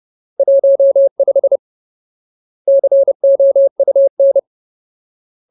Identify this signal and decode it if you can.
Morse code, heard as a keyed tone, about 30 wpm, 550 Hz. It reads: 15 COUN
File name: Ultimate Word List / 30WPM Ultimate